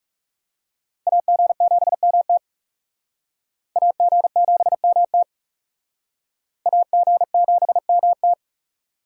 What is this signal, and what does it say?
Morse code, heard as a keyed tone, about 36 wpm, 700 Hz. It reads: AG7MT AG7MT AG7MT
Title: Speed Racing - USA 2x2 Call Signs 30wpm